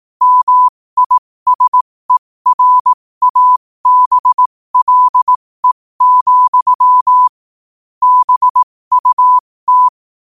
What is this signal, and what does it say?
Morse code, heard as a keyed tone, about 18 wpm, spaced as for 15 wpm, 1000 Hz. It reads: MISERABLE, BUT
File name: qotdfiles / 15wpm